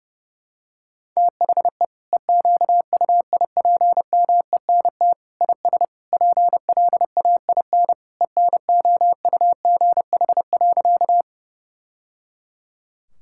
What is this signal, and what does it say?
Morse code, heard as a keyed tone, about 30 wpm, 700 Hz. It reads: THE EQUIPMENT IS PLAIN ENOUGH.
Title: Sentences from Top 1000 Words - Review Part 5 - 30wpm